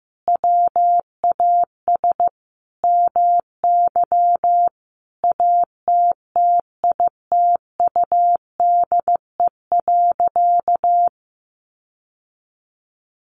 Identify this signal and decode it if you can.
Morse code, heard as a keyed tone, about 15 wpm, 700 Hz. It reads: WAS MY ATTITUDE.
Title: Sentences from Top 1300 Words 15wpm